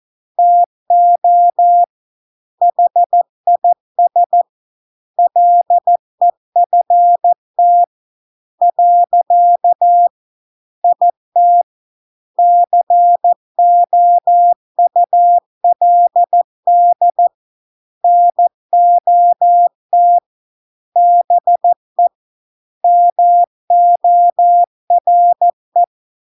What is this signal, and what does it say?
Morse code, heard as a keyed tone, about 14 wpm, 700 Hz. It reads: TO HIS LEFT . IT COULD NOT BE MORE